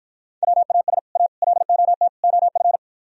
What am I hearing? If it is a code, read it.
Morse code, heard as a keyed tone, about 52 wpm, 700 Hz. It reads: WAS A LOT OF